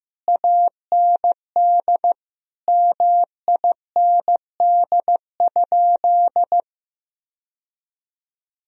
Morse code, heard as a keyed tone, about 15 wpm, 700 Hz. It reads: AND MIND?
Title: Sentences from Top 1000 Words - Review Part 2 - 15wpm